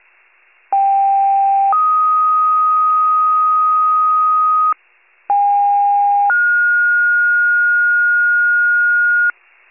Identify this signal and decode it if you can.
Two-tone page, heard as a one-second tone, then a three-second tone, about 790 then 1250 Hz, about 810 then 1430 Hz, with noise. It is heard from 0.7 to 9.3 s.
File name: wcalert